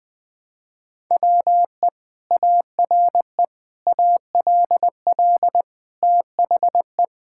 Morse code, heard as a keyed tone, about 20 wpm, 700 Hz. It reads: WE ARE ALL THE
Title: Sentences with 2-Word Phrases - 20wpm